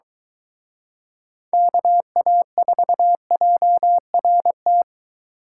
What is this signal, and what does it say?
Morse code, heard as a keyed tone, about 22 wpm, 700 Hz. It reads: KA4JRT